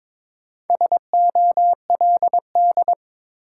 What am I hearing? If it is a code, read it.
Morse code, heard as a keyed tone, about 22 wpm, 700 Hz. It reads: SOLD
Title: Sentences from Top 1400 Words - New Word Drill Alt - 22wpm